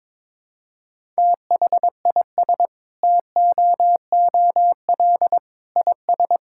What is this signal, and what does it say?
Morse code, heard as a keyed tone, about 22 wpm, 700 Hz. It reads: THIS TOOL IS
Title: Sentences from Top 1000 Words - Review Part 6 - 22wpm